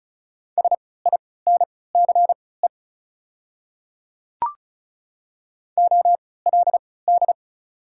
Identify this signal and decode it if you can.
Morse code, heard as a keyed tone, about 36 wpm, spaced as for 20 wpm, 700 Hz. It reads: SINCE OLD
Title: 3x Character Spacing - Top 300 Words - 35wpm